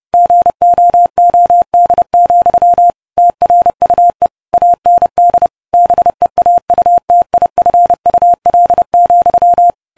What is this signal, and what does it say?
Morse code, heard as a keyed tone, about 30 wpm, 700 Hz. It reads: GOOD, TRUE AND BEAUTIFUL,